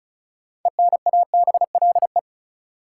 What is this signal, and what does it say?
Morse code, heard as a keyed tone, about 34 wpm, 700 Hz. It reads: ENABLE
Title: Sentences from Top 1500 Words - New Word Drill Alt - 35wpm